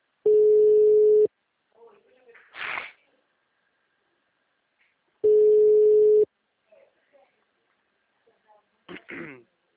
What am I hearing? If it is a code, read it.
Ringback tone (Europe, 425 Hz), with speech.